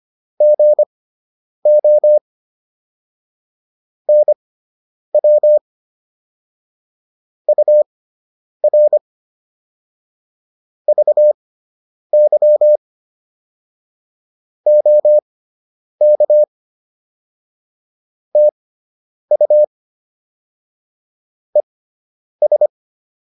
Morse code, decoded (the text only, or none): GO NW UR VY OK TU ES